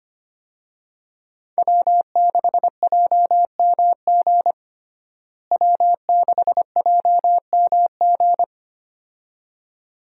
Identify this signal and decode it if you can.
Morse code, heard as a keyed tone, about 25 wpm, 700 Hz. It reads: W6JMG W6JMG
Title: USA Call Signs - 5 Characters - Repeated 2x - 25wpm